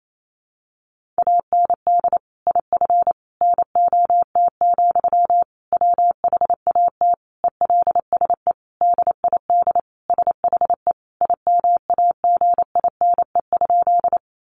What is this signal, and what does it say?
Morse code, heard as a keyed tone, about 28 wpm, 700 Hz. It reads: AND IF NOT, WHAT ELSE DID SHE IMAGINE?
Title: Sentences from Top 1400 Words 28wpm